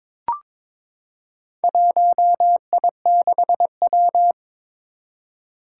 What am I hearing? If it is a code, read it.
Morse code, heard as a keyed tone, about 22 wpm, 700 Hz. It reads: 1I6W
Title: Four Challenging Letter-Number Combinations 22wpm